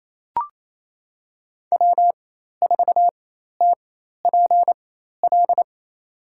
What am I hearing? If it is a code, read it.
Morse code, heard as a keyed tone, about 28 wpm, spaced as for 13 wpm, 700 Hz. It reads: W4TPL
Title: USA Call Signs - 5 Characters - 4x Character Spacing - 28wpm